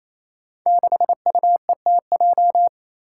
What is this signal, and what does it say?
Morse code, heard as a keyed tone, about 27 wpm, 700 Hz. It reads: 6UETJ
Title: Five Challenging Letter-Number Combinations 28wpm